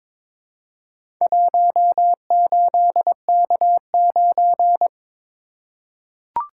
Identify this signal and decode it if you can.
Morse code, heard as a keyed tone, about 23 wpm, 700 Hz. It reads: 18K9
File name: practice / 22wpm / Four Letter-Number Combinations - 22wpm